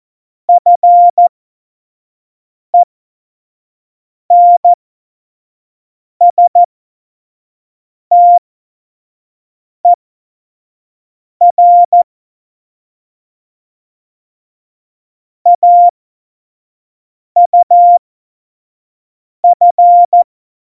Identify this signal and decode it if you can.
Morse code, heard as a keyed tone, about 14 wpm, spaced as for 5 wpm, 700 Hz. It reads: FENSTER AUF
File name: Morsekurs / Witze / Witz 17.Juli 2018, 25 F70